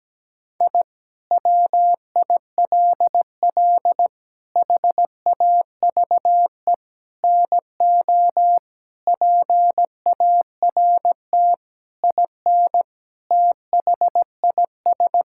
Morse code, heard as a keyed tone, about 17 wpm, 700 Hz. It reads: I WILL HAVE NO PART IN THIS
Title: Sentences from Top 1000 Words - Review Part 5 - 17wpm